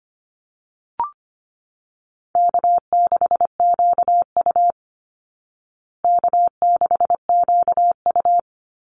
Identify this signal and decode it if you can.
Morse code, heard as a keyed tone, about 25 wpm, 700 Hz. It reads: K6QU K6QU
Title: Easy Call Signs Repeated 2x - 25wpm